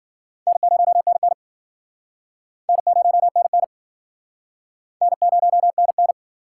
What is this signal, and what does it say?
Morse code, heard as a keyed tone, about 52 wpm, 700 Hz. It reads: N0NN N0NN N0NN